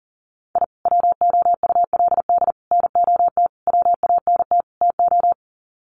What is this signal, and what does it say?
Morse code, heard as a keyed tone, about 40 wpm, 700 Hz. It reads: I WOULD NOT WANT TO
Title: Sets of 5 Words 40wpm